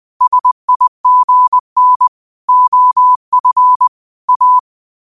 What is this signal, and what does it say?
Morse code, heard as a keyed tone, about 20 wpm, 1000 Hz. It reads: SIGN OF A